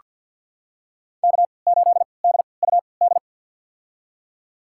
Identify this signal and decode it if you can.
Morse code, heard as a keyed tone, about 48 wpm, spaced as for 28 wpm, 700 Hz. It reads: K8DUD
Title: USA Call Signs - 5 Characters - 3x Character Spacing - 50wpm